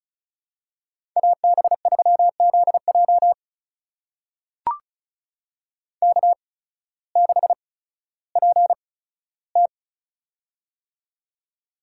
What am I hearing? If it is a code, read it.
Morse code, heard as a keyed tone, about 34 wpm, spaced as for 16 wpm, 700 Hz. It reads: AB3ZJ K6PT
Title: USA Call Signs - 8x Character Spacing - 35wpm